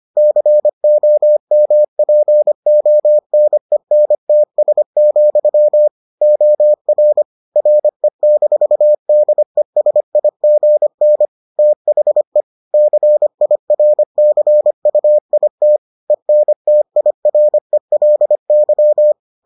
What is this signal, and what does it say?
Morse code, heard as a keyed tone, about 25 wpm, 600 Hz. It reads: COMPONENTS, OR RE-DESIGN THE CIRCUIT ENTIRELY